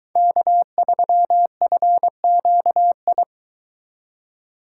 Morse code, heard as a keyed tone, about 23 wpm, 700 Hz. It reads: K3FQI